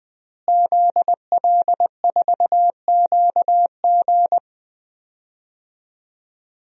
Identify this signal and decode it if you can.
Morse code, heard as a keyed tone, about 21 wpm, 700 Hz. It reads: ZL4QG
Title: DXCC-100 Call Signs - 5 Characters - Repeated 2x - 20wpm